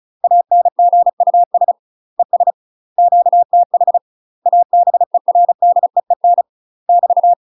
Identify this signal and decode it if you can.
Morse code, heard as a keyed tone, about 35 wpm, 700 Hz. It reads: ANGUS ES QTH ABERDEEN <BT>
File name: UK MASTER QSOs TRAINING / 35WPM / UK MASTER QSOs 0000